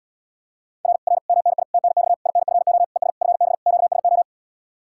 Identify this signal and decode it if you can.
Morse code, heard as a keyed tone, about 80 wpm, 700 Hz. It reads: R R MNI TNX INFO ES FB QSO